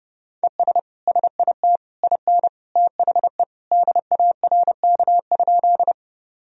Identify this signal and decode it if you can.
Morse code, heard as a keyed tone, about 30 wpm, 700 Hz. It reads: ES SIT IN THE DARK?
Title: Sentences from Top 500 Words and Abbreviations 30wpm